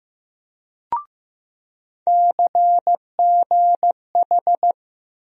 Morse code, heard as a keyed tone, about 15 wpm, 700 Hz. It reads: CGH
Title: USA Call Sign Postfixes - 3 Letters 15wpm